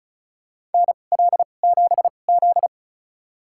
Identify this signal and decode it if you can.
Morse code, heard as a keyed tone, about 34 wpm, spaced as for 25 wpm, 700 Hz. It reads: NL7Z
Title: USA Call Signs - 4 Characters - 2x Character Spacing - 35wpm